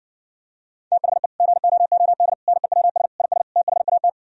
Morse code, heard as a keyed tone, about 59 wpm, 700 Hz. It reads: THE GOOD NEWS IS THAT